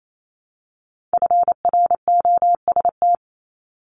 Morse code, heard as a keyed tone, about 28 wpm, 700 Hz. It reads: FROST
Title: Multi-letter Phonemes Repeated 3x 28wpm